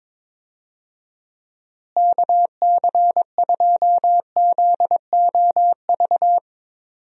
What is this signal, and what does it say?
Morse code, heard as a keyed tone, about 22 wpm, 700 Hz. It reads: KC2ZOV
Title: USA Call Signs - 6 Characters - Repeated 2x - 22wpm